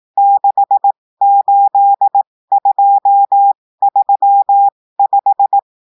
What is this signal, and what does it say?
Morse code, heard as a keyed tone, about 18 wpm, spaced as for 15 wpm, 800 Hz. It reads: 68235
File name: cours-du-lundi files / cours-1580-texte-chiffre-15-mpm